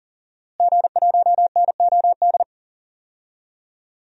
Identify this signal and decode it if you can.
Morse code, heard as a keyed tone, about 40 wpm, 700 Hz. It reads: G1NOD
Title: DXCC-100 Call Signs - 5 Characters - Repeated 2x - 40wpm